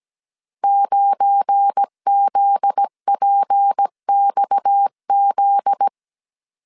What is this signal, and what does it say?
Morse code, heard as a keyed tone, about 17 wpm, 800 Hz. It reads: 9ZPXZ
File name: CW / DET / 1 9 1500C S C 16wpm 2909